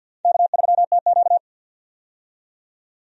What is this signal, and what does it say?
Morse code, heard as a keyed tone, about 48 wpm, 700 Hz. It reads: K3TQ
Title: USA Call Signs - 4 Characters - 50wpm